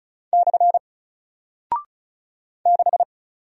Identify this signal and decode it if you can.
Morse code, heard as a keyed tone, about 34 wpm, 700 Hz. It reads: / 6
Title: Single Character (Letter:Number:Punctuation:Prosign) Alt 35wpm